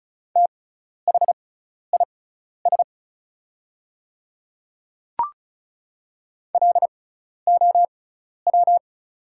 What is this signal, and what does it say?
Morse code, heard as a keyed tone, about 36 wpm, spaced as for 12 wpm, 690 Hz. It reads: THIS LOW
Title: 6x Character Spacing - Top 200 Words - 35wpm